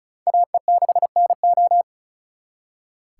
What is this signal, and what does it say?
Morse code, heard as a keyed tone, about 46 wpm, 700 Hz. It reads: AE6NO